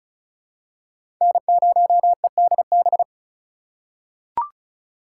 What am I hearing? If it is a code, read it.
Morse code, heard as a keyed tone, about 36 wpm, 700 Hz. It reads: N0EDB
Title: USA Call Signs - 2x Character Spacing - 35wpm